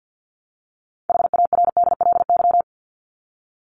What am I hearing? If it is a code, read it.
Morse code, heard as a keyed tone, about 50 wpm, 700 Hz. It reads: HARDLY
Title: Top 1751-2000 Words 50wpm